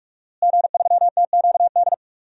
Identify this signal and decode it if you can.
Morse code, heard as a keyed tone, about 47 wpm, 700 Hz. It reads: G3TQD
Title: DXCC-100 Call Signs - 5 Characters - Repeated 3x - 45wpm